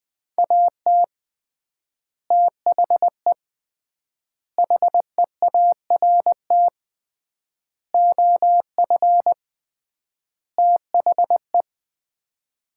Morse code, heard as a keyed tone, about 20 wpm, 700 Hz. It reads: AT THE HEART OF THE
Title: 3x Word-Spacing - Sets of 5 Words - 20wpm